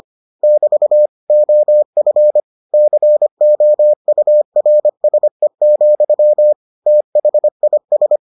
Morse code, heard as a keyed tone, about 25 wpm, 600 Hz. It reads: <BT> OF COURSE, THIS